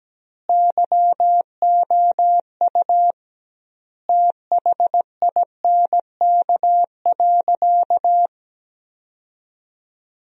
Morse code, heard as a keyed tone, about 17 wpm, 700 Hz. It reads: YOU THINK.
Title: 2x Word-Spacing - Sentences from Top 400 Words - 17wpm